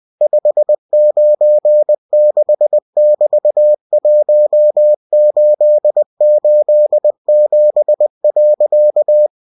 Morse code, heard as a keyed tone, about 20 wpm, 600 Hz. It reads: 596<BT>1887.
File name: cw-books / winnie-the-pooh / winnie-20wpm-page-0121